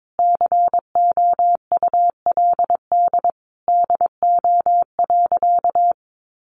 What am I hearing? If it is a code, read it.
Morse code, heard as a keyed tone, about 22 wpm, 700 Hz. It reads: COULD DO.
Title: Sentences from Top 1400 Words 22wpm